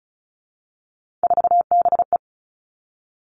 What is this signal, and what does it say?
Morse code, heard as a keyed tone, about 36 wpm, 700 Hz. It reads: 4BE
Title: Three Challenging Letter-Number Combinations 35wpm